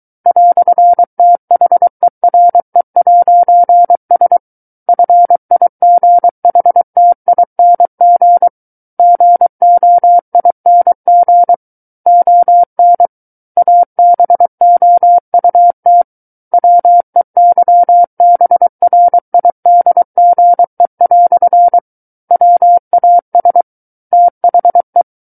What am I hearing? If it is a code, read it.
Morse code, heard as a keyed tone, about 23 wpm, 700 Hz. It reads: "THERE'S FIGHTING GOING ON ABOUT WEYBRIDGE" WAS THE
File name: Book One / 14-Chapter 14 - 23 WPM